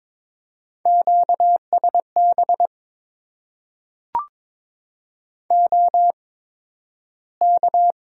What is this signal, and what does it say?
Morse code, heard as a keyed tone, about 22 wpm, spaced as for 12 wpm, 700 Hz. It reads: QSB OK
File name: Top 90 QSO Elements - 8x Character Spacing - 22wpm